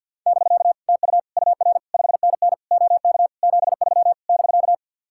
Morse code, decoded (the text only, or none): <BK> TU UR 5NN OK 73 <BK>